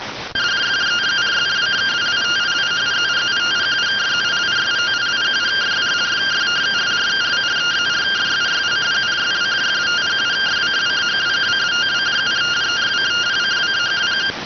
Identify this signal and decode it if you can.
Two-tone teleprinter signal, about 170 Hz shift.